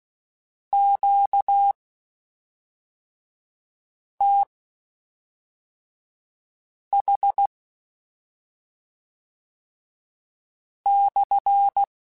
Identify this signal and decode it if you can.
Morse code, heard as a keyed tone, about 16 wpm, spaced as for 3 wpm, 800 Hz. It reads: QTH/
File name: Q Calls / Q Calls-mp3s / Q Calls SPSL1 @5